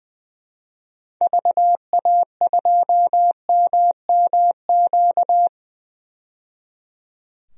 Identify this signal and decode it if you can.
Morse code, heard as a keyed tone, about 20 wpm, 700 Hz. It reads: VA2MMQ
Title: DXCC-100 Call Signs - 6 Characters - 20wpm